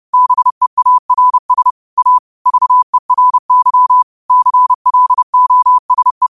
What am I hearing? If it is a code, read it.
Morse code, heard as a keyed tone, about 30 wpm, 1000 Hz. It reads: BEARS A VERY CLOSE